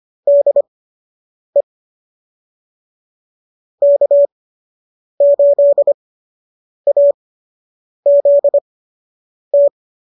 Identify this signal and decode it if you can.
Morse code, heard as a keyed tone, about 25 wpm, spaced as for 8 wpm, 590 Hz. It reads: DE K8AZT
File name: qso006 08